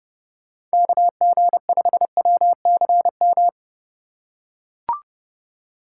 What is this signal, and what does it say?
Morse code, heard as a keyed tone, about 30 wpm, 700 Hz. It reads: KG5WCM
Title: USA Call Signs - 6 Characters - 4x Character Spacing - 30wpm